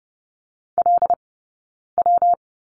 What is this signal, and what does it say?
Morse code, heard as a keyed tone, about 30 wpm, spaced as for 9 wpm, 690 Hz. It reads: LW
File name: Lesson 19d - Letter w - Two Characters 30wpm